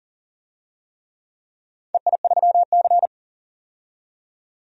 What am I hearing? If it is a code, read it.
Morse code, heard as a keyed tone, about 39 wpm, 700 Hz. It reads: EI3C
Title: DXCC-100 Call Signs - 4 Characters - 40wpm